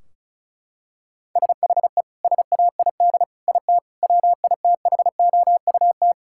Morse code, heard as a keyed tone, about 34 wpm, 700 Hz. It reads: SHE SAID IT WITHOUT
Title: Sentences from Top 1600 Words 35wpm